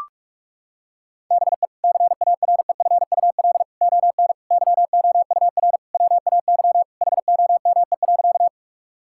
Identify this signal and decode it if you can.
Morse code, heard as a keyed tone, about 41 wpm, 700 Hz. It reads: BE CAREFUL ON YOUR WAY HOME.
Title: Sentences from Top 1600 Words 45wpm